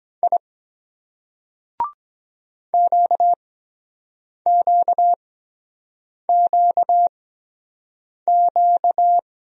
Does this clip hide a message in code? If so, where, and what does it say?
Yes: Morse code from 0.2 to 9.2 s, heard as a keyed tone, about 23 wpm, 700 Hz. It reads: I Q Q Q Q